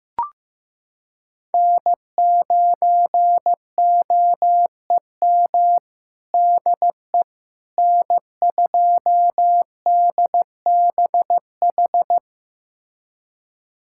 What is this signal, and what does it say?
Morse code, heard as a keyed tone, about 15 wpm, 700 Hz. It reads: N9OEM DE N2DBH
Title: USA Callsign DE Callsign - 5 Characters - 15wpm